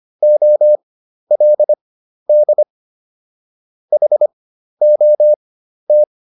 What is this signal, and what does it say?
Morse code, heard as a keyed tone, about 25 wpm, spaced as for 12 wpm, 600 Hz. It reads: OLD HOT